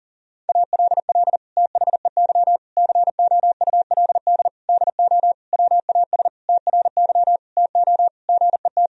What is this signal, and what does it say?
Morse code, heard as a keyed tone, about 37 wpm, 700 Hz. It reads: ALL THEY COULD DO WAS TRY TO GET
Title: Sentences from Top 1000 Words - Review Part 1 - 40wpm